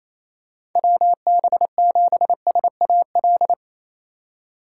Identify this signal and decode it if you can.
Morse code, heard as a keyed tone, about 28 wpm, 700 Hz. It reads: WB7SAL
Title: USA Call Signs - 3x Character Spacing - 28wpm